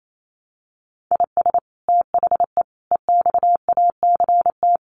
Morse code, heard as a keyed tone, about 28 wpm, 700 Hz. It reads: IS THE EXACT
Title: Sentences from Top 1300 Words 28wpm